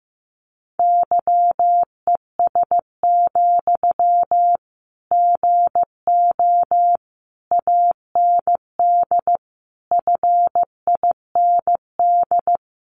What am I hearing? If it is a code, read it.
Morse code, heard as a keyed tone, about 15 wpm, 700 Hz. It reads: YES, GO AND FIND
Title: Sentences from Top 500 Words 15wpm